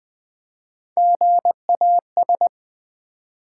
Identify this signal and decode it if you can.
Morse code, heard as a keyed tone, about 20 wpm, 700 Hz. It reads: GAS
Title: Three Letter Words - 20wpm